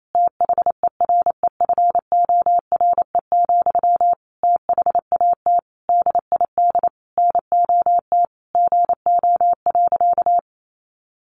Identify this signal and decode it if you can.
Morse code, heard as a keyed tone, about 28 wpm, 700 Hz. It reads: THEREFORE, THAT DID NOT GO.
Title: Sentences from Top 800 Words 28wpm